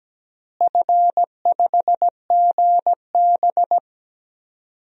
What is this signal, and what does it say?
Morse code, heard as a keyed tone, about 17 wpm, 700 Hz. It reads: F5GB
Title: DXCC-100 Call Signs - 4 Characters - Repeated 3x - 17wpm